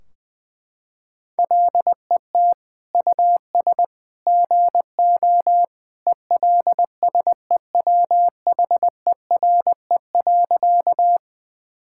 Morse code, heard as a keyed tone, about 20 wpm, 700 Hz. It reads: LET US GO ELSEWHERE.
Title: Sentences from Top 1600 Words 20wpm